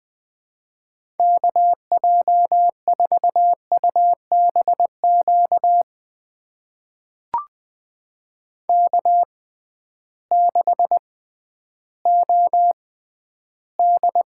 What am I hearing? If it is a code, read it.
Morse code, heard as a keyed tone, about 20 wpm, spaced as for 12 wpm, 700 Hz. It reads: KJ4UBQ K6OD
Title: USA Call Signs - 6x Character Spacing - 20wpm